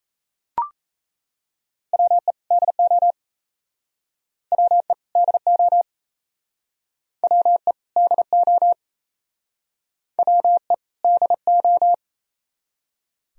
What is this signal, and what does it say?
Morse code, heard as a keyed tone, about 37 wpm, 700 Hz. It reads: WE DO WE DO WE DO WE DO